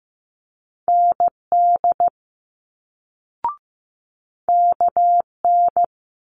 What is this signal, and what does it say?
Morse code, heard as a keyed tone, about 15 wpm, 700 Hz. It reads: ND KN
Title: Multi-letter Phonemes Quick Fire 15wpm